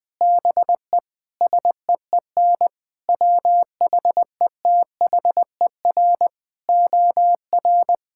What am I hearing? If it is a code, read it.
Morse code, heard as a keyed tone, about 20 wpm, 700 Hz. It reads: BE SEEN WHETHER OR
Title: Sets of 8 Words - 20wpm